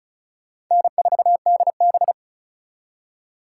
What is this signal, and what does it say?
Morse code, heard as a keyed tone, about 44 wpm, 700 Hz. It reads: N4DB
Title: Speed Racing - USA 1x2 Call Signs 35wpm